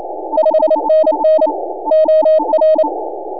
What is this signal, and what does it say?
Morse code, heard as a keyed tone, about 29 wpm, 610 Hz, with noise. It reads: 5NN OR
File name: n6tr / N6AA Skimmed